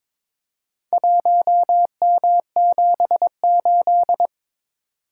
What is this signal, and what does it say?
Morse code, heard as a keyed tone, about 22 wpm, 700 Hz. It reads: 1M78
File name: Four Letter-Number Combinations - 22wpm